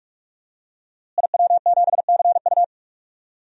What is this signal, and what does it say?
Morse code, heard as a keyed tone, about 47 wpm, 700 Hz. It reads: IW7CU
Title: DXCC-100 Call Signs - 5 Characters - Repeated 2x - 45wpm